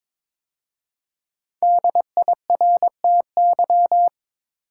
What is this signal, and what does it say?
Morse code, heard as a keyed tone, about 22 wpm, 700 Hz. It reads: DIRTY
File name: 5 Letter Words - First 3 Letters Given - 22wpm